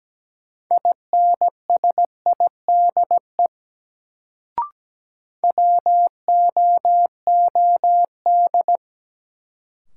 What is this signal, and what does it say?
Morse code, heard as a keyed tone, about 17 wpm, 700 Hz. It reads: INSIDE WOOD
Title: Sentences from Top 1000 Words - Review Part 2 - New Word Drill - 17wpm